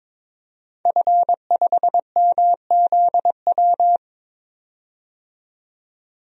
Morse code, heard as a keyed tone, about 22 wpm, 700 Hz. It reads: F5MZW
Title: DXCC-100 Call Signs - 5 Characters - 22wpm